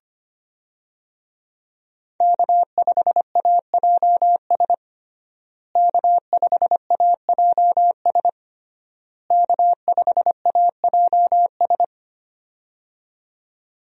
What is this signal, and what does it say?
Morse code, heard as a keyed tone, about 25 wpm, 700 Hz. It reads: K5AJS K5AJS K5AJS
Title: USA Call Signs - 5 Characters - Repeated 3x - 25wpm